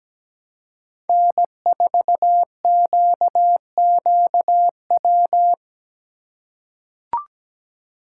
Morse code, heard as a keyed tone, about 17 wpm, 700 Hz. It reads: N4QQW
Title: USA Call Signs - 5 Characters - 4x Character Spacing - 17wpm